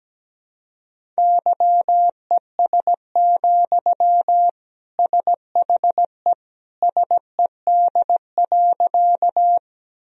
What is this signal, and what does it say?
Morse code, heard as a keyed tone, about 17 wpm, 700 Hz. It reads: YES, SHE SED.
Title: Sentences from Top 500 Words and Abbreviations 17wpm